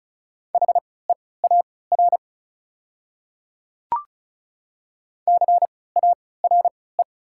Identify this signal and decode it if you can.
Morse code, heard as a keyed tone, about 34 wpm, spaced as for 20 wpm, 700 Hz. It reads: HEAR CARE
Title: 3x Character Spacing - Top 300 Words - 35wpm